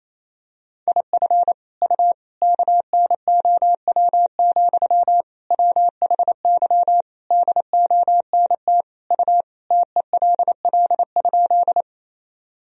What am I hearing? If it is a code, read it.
Morse code, heard as a keyed tone, about 28 wpm, 700 Hz. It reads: IF U KNOW, WHY DONT U TELL?
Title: Sentences from Top 300 Words and Abbreviations 28wpm